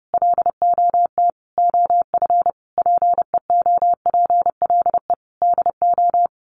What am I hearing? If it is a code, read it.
Morse code, heard as a keyed tone, about 30 wpm, 700 Hz. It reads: LOT OF PEOPLE DO
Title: Sets of 5 Words 30wpm